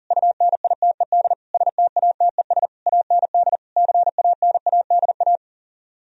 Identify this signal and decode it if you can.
Morse code, heard as a keyed tone, about 50 wpm, 700 Hz. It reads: UNITED STATES AND CANADA
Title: Speed-Racing - Sets of 5 Words 40wpm